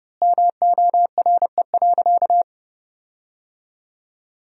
Morse code, heard as a keyed tone, about 32 wpm, 700 Hz. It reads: MORE.
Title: 4x Word-Spacing - Sentences from Top 500 Words - 30wpm